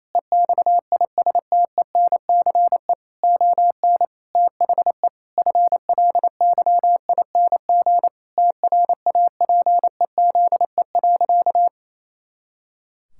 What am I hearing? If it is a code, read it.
Morse code, heard as a keyed tone, about 28 wpm, 700 Hz. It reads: EXISTENCE ON THE FLYING TRAPEZE.
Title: Pangram Challenge v2 - 28wpm